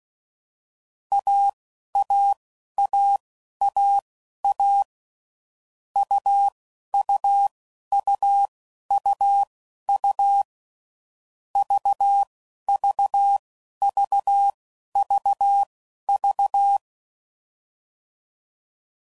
Morse code, heard as a keyed tone, about 16 wpm, spaced as for 11 wpm, 800 Hz. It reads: AAAAA UUUUU VVVVV
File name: Mixed / Short Runs / AUV AUV4